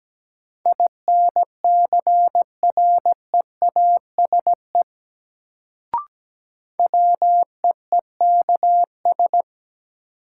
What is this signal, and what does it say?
Morse code, heard as a keyed tone, about 17 wpm, 700 Hz. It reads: INCREASE WEEKS
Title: Sets of 5 Words - New Word Drill - 17wpm